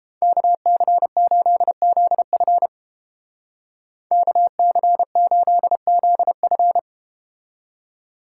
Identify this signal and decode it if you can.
Morse code, heard as a keyed tone, about 33 wpm, 700 Hz. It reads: KC8ZF KC8ZF